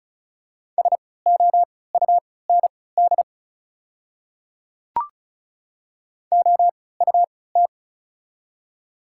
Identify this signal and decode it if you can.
Morse code, heard as a keyed tone, about 36 wpm, spaced as for 20 wpm, 690 Hz. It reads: SOUND OUT
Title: 3x Character Spacing - Top 100 Words - 35wpm